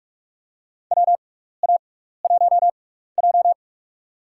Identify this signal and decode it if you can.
Morse code, heard as a keyed tone, about 44 wpm, spaced as for 15 wpm, 700 Hz. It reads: WA1J
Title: USA Call Signs - 4 Characters - 6x Character Spacing - 45wpm